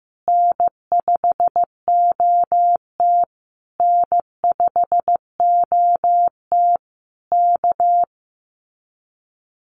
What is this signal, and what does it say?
Morse code, heard as a keyed tone, about 14 wpm, 700 Hz. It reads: N5OT N5OT K
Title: Common QSO Phrases Encore 15wpm